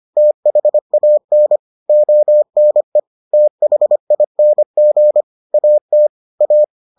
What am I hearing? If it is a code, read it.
Morse code, heard as a keyed tone, about 25 wpm, 600 Hz. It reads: THAN ONE THING AT A